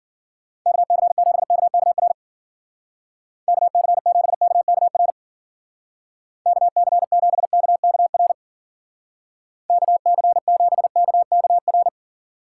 Morse code, heard as a keyed tone, about 49 wpm, 700 Hz. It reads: KC7KKR KC7KKR KC7KKR KC7KKR
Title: Speed Racing - USA 2x3 Call Signs 40wpm